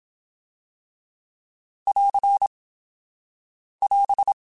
Morse code, heard as a keyed tone, about 27 wpm, 800 Hz. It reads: <AR> <AS>
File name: Alphabetical / Alphabetical mp3s / Ss @25